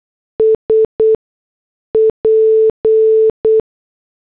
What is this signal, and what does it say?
Morse code, heard as a keyed tone, about 8 wpm, spaced as for 6 wpm, 440 Hz. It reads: SP